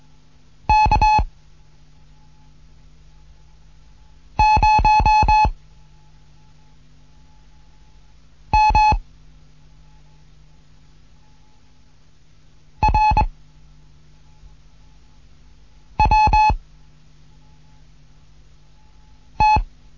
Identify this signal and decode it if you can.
Morse code, heard as a keyed tone, about 21 wpm, 860 Hz. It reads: K 0 M R W T
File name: Morsen / Kurs A 15